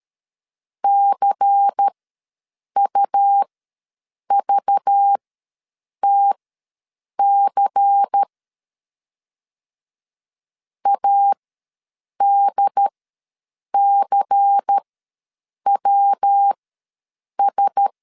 Morse code, heard as a keyed tone, about 13 wpm, spaced as for 7 wpm, 800 Hz. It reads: CUVTC ADCWS
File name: LEZIONI / 5 LBVC / 5 LBVC ws 12wpm 3159